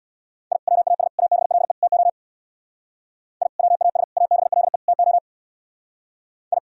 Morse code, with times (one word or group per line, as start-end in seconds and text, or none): 0.5-0.6 I
0.7-1.1 WAS
1.2-1.7 ABLE
1.8-2.1 TO
3.4-3.5 I
3.6-4.0 WAS
4.2-4.8 ABLE
4.9-5.2 TO
6.5-6.6 I